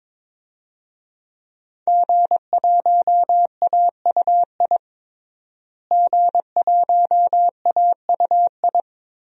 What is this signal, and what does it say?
Morse code, heard as a keyed tone, about 22 wpm, 700 Hz. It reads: G1AUI G1AUI
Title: DXCC-100 Call Signs - 5 Characters - Repeated 2x - 22wpm